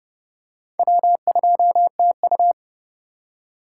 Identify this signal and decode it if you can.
Morse code, heard as a keyed tone, about 30 wpm, 700 Hz. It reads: W2TU
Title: USA Call Signs - 3x Character Spacing - 30wpm